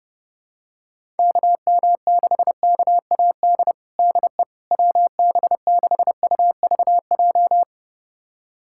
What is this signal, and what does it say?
Morse code, heard as a keyed tone, about 30 wpm, 700 Hz. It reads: KM6KAD DE WB6UVJ
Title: USA Callsign DE Callsign - 6 Characters - 30wpm